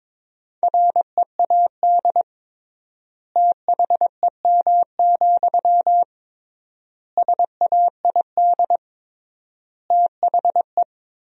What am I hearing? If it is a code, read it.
Morse code, heard as a keyed tone, about 22 wpm, 700 Hz. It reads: READ THEM, SAID THE
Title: 3x Word-Spacing - Sentences from Top 400 Words - 22wpm